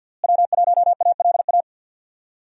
Morse code, heard as a keyed tone, about 48 wpm, 700 Hz. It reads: W1ALA
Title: USA Call Signs - 5 Characters - Repeated 3x - 50wpm